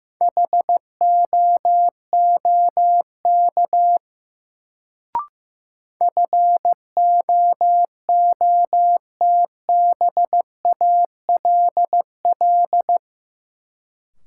Morse code, heard as a keyed tone, about 15 wpm, 700 Hz. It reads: HOOK FOOTBALL
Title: Sentences from Top 1600 Words - New Word Drill Alt - 15wpm